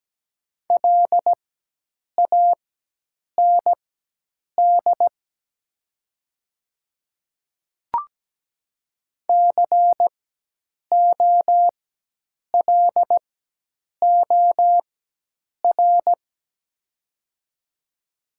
Morse code, heard as a keyed tone, about 17 wpm, spaced as for 8 wpm, 700 Hz. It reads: LAND COLOR
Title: 4x Character Spacing - Top 300 Words - 17wpm